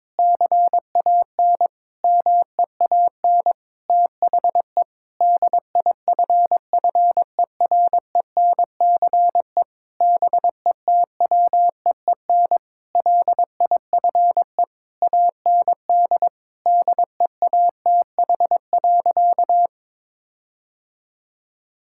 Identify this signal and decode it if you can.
Morse code, heard as a keyed tone, about 22 wpm, 700 Hz. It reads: CAN MEAN THE DIFFERENCE BETWEEN LIFE AND DEATH.
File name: practice / 22wpm / Sentences with 8-Word Phrases - 22wpm